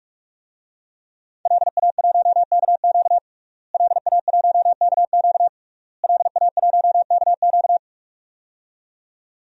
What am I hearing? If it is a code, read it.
Morse code, heard as a keyed tone, about 43 wpm, 700 Hz. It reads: LA1KQ LA1KQ LA1KQ
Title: DXCC-100 Call Signs - 5 Characters - Repeated 3x - 45wpm